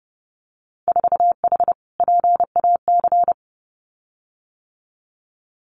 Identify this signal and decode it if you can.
Morse code, heard as a keyed tone, about 30 wpm, 700 Hz. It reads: 4H PAC
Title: Winter Field Day Exchanges Encore 30wpm